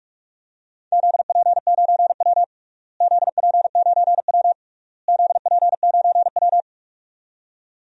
Morse code, heard as a keyed tone, about 46 wpm, 700 Hz. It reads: ZP9W ZP9W ZP9W
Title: DXCC-100 Call Signs - 4 Characters - Repeated 3x - 45wpm